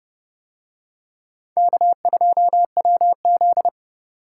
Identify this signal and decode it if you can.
Morse code, heard as a keyed tone, about 30 wpm, 700 Hz. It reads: K2WZ